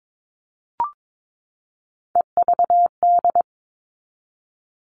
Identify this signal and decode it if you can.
Morse code, heard as a keyed tone, about 22 wpm, 700 Hz. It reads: EVD
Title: All Cut Numbers - 3-Digit 22wpm